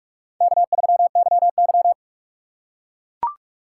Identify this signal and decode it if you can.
Morse code, heard as a keyed tone, about 44 wpm, 690 Hz. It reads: K3YY
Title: USA Call Signs - 4 Characters - 3x Character Spacing - 45wpm